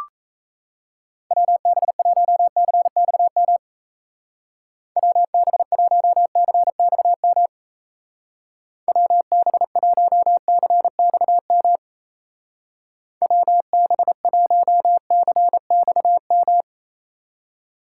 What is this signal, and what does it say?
Morse code, heard as a keyed tone, about 35 wpm, 700 Hz. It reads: WB1CXM WB1CXM WB1CXM WB1CXM
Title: Speed Racing - USA 2x3 Call Signs 28wpm